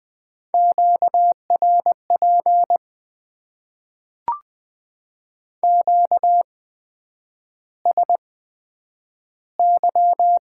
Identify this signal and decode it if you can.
Morse code, heard as a keyed tone, about 20 wpm, spaced as for 9 wpm, 700 Hz. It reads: QRP QSY
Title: Top 90 QSO Elements - 8x Character Spacing - 20wpm